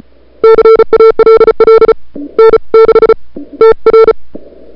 Morse code, heard as a keyed tone, about 35 wpm, spaced as for 27 wpm, 440 Hz. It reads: CALL N6TR
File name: CWSprint / February 2007 CW Sprint / 01-2007 CW Sprint Hour1